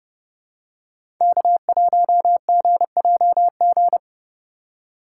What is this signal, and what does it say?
Morse code, heard as a keyed tone, about 30 wpm, 700 Hz. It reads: K1GJG
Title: USA Call Signs - 2x Character Spacing - 30wpm